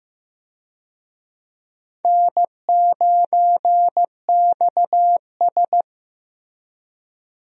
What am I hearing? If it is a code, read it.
Morse code, heard as a keyed tone, about 15 wpm, 700 Hz. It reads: N9XS
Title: USA Call Signs - 4 Characters - Repeated 3x - 15wpm